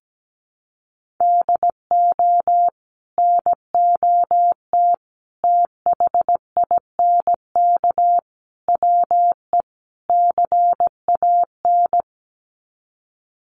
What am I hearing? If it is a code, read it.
Morse code, heard as a keyed tone, about 17 wpm, 700 Hz. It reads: DO NOT THINK WE CAN
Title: Sets of 5 Words Encore2 17wpm